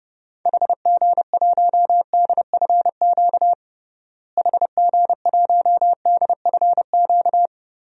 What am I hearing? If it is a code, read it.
Morse code, heard as a keyed tone, about 32 wpm, 700 Hz. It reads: HG1DFQ HG1DFQ
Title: DXCC-100 Call Signs - 6 Characters - Repeated 3x - 30wpm